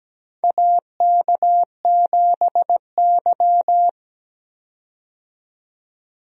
Morse code, heard as a keyed tone, about 17 wpm, 700 Hz. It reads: AK7Y
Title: USA Call Signs - 4 Characters - 17wpm